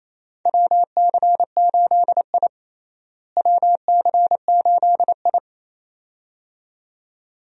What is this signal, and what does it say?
Morse code, heard as a keyed tone, about 28 wpm, 700 Hz. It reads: WC8I WC8I
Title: USA Call Signs - 4 Characters - Repeated 2x - 28wpm